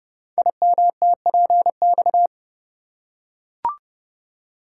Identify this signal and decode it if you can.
Morse code, heard as a keyed tone, about 30 wpm, 700 Hz. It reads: IMTPX
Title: Code Groups - 5 Letters - 30wpm